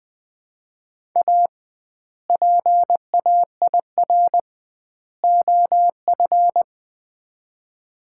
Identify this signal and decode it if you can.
Morse code, heard as a keyed tone, about 20 wpm, 700 Hz. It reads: A PAIR OF
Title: 2x Word-Spacing - Sets of 3 Words - 20wpm